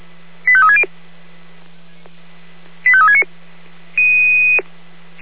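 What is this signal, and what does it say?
Five-tone selective call (ZVEI), heard as quick runs of five tones, tones of about 70 ms. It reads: 85360 85369
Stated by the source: ZVEI five-tone selective call 85360, received by SDR, from the feuerwehr-orlamuende.org (Feuerwehr Orlamünde) server.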